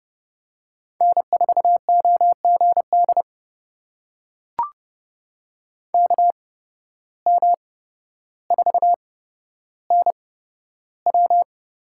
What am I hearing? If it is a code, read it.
Morse code, heard as a keyed tone, about 30 wpm, spaced as for 13 wpm, 700 Hz. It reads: N4OGD KM4NW
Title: USA Call Signs - 5 Characters - 8x Character Spacing - 30wpm